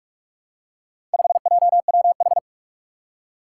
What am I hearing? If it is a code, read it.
Morse code, heard as a keyed tone, about 45 wpm, 700 Hz. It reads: 5JWH
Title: Four Challenging Letter-Number Combinations 45wpm